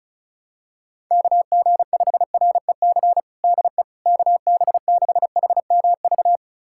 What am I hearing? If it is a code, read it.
Morse code, heard as a keyed tone, about 34 wpm, 700 Hz. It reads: KG5REC DE KB6HMV
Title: USA Callsign DE Callsign - 6 Characters - 35wpm